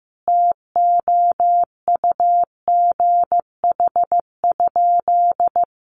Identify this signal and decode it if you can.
Morse code, heard as a keyed tone, about 15 wpm, 700 Hz. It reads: TOUGH?
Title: Sentences from Top 1200 Words 15wpm